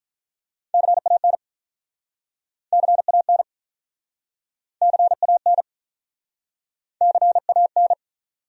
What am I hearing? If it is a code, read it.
Morse code, heard as a keyed tone, about 48 wpm, 700 Hz. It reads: CAN CAN CAN CAN